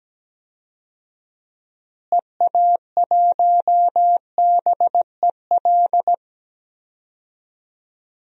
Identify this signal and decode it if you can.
Morse code, heard as a keyed tone, about 17 wpm, 700 Hz. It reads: EA1BEL
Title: DXCC-100 Call Signs - 6 Characters - Repeated 2x - 17wpm